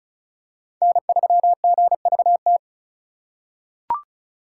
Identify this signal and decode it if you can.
Morse code, heard as a keyed tone, about 34 wpm, 700 Hz. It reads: N3GVT
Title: USA Call Signs - 5 Characters - 4x Character Spacing - 35wpm